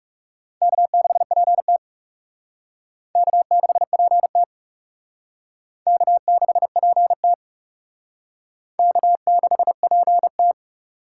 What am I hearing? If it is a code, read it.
Morse code, heard as a keyed tone, about 35 wpm, 700 Hz. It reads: K6PT K6PT K6PT K6PT
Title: Speed Racing - USA 1x2 Call Signs 30wpm